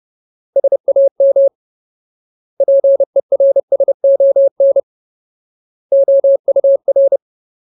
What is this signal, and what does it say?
Morse code, heard as a keyed tone, about 30 wpm, 550 Hz. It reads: SAM PERSON OUR
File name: Ultimate Word List / 30WPM Ultimate